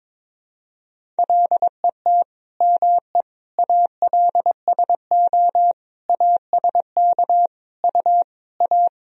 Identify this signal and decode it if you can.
Morse code, heard as a keyed tone, about 22 wpm, 700 Hz. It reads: LET ME ALSO ASK U A
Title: Sentences from Top 400 Words and Abbreviations 22wpm